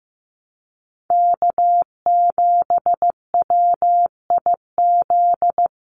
Morse code, heard as a keyed tone, about 15 wpm, 700 Hz. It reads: K7WIZ
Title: USA Call Signs Repeated 2x 15wpm